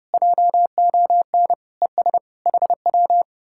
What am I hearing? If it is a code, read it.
Morse code, heard as a keyed tone, about 33 wpm, 700 Hz. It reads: JON ES HW